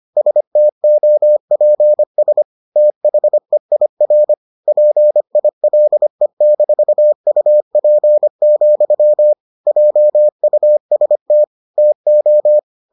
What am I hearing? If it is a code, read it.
Morse code, heard as a keyed tone, about 25 wpm, 600 Hz. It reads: STOPS THEIR PILE-UP, JUST TO